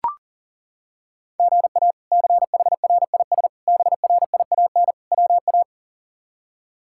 Morse code, heard as a keyed tone, about 40 wpm, 700 Hz. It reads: GA CHRIS BRIAN WA
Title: K1USN SST QSO Practice - 40wpm